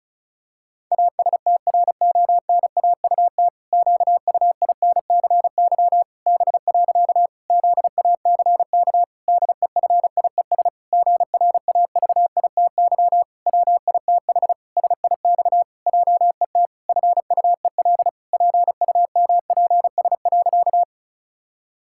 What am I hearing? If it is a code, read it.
Morse code, heard as a keyed tone, about 36 wpm, 700 Hz. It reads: ASTRONAUT QUINCY B. ZACK DEFIES GRAVITY WITH SIX JET FUEL PUMPS.
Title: Pangram Challenge v2 - 35wpm